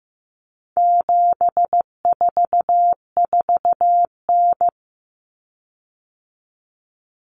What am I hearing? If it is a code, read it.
Morse code, heard as a keyed tone, about 15 wpm, 700 Hz. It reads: 744N
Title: Common Cut Numbers - 4-Digit 15wpm